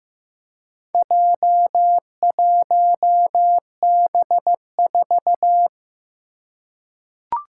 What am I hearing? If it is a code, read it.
Morse code, heard as a keyed tone, about 15 wpm, 700 Hz. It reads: J1B4
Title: Four Challenging Letter-Number Combinations 15wpm